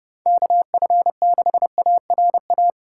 Morse code, heard as a keyed tone, about 32 wpm, 700 Hz. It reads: KF6ARA
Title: USA Call Signs - 6 Characters - 4x Character Spacing - 30wpm